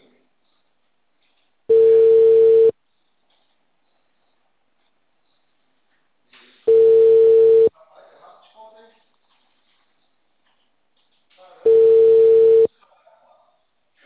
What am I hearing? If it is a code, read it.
Ringback tone (China, 450 Hz), with speech.